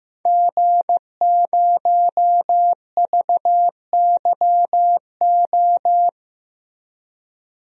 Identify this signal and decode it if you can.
Morse code, heard as a keyed tone, about 16 wpm, 700 Hz. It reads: G0VYO
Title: DXCC-100 Call Signs - 5 Characters - Repeated 3x - 15wpm